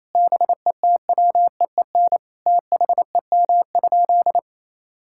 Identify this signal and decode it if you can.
Morse code, heard as a keyed tone, about 28 wpm, 700 Hz. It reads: BETWEEN THEM?
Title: Sentences from Top 1000 Words - Review Part 6 - 28wpm